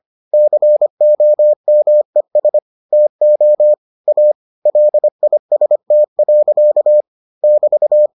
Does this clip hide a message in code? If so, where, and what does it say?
Yes: Morse code from 0.3 to 8.1 s, heard as a keyed tone, about 25 wpm, 600 Hz. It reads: COMES TO A LIST. <BT>